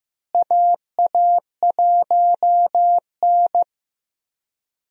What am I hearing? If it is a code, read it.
Morse code, heard as a keyed tone, about 15 wpm, 700 Hz. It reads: AA1N
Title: USA Call Signs - 4 Characters - Repeated 3x - 15wpm